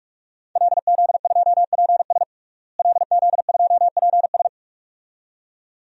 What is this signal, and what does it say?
Morse code, heard as a keyed tone, about 46 wpm, 700 Hz. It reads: LZ2PS LZ2PS
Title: DXCC-100 Call Signs - 5 Characters - Repeated 2x - 45wpm